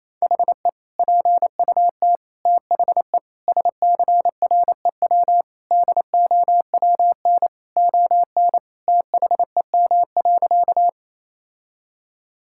Morse code, heard as a keyed tone, about 28 wpm, 700 Hz. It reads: HE PUT THE SCREW DOWN ON THEM.
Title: Sentences from Top 1600 Words 28wpm